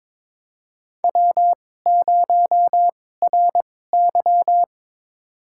Morse code, heard as a keyed tone, about 21 wpm, spaced as for 16 wpm, 700 Hz. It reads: W0RY
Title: USA Call Signs - 4 Characters - 2x Character Spacing - 22wpm